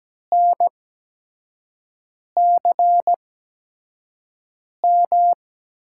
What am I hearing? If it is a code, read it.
Morse code, heard as a keyed tone, about 17 wpm, spaced as for 5 wpm, 700 Hz. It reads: NCM
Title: USA Call Signs - 8x Character Spacing - 17wpm